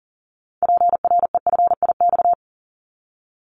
Morse code, heard as a keyed tone, about 40 wpm, 700 Hz. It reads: PREFIX
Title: Callsign Prefixes for Top 25 DXCC - 40wpm